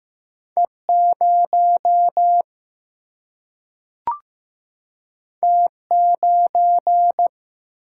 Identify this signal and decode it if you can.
Morse code, heard as a keyed tone, about 15 wpm, 700 Hz. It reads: E0 T9
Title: Single-Letter Single-Number 15wpm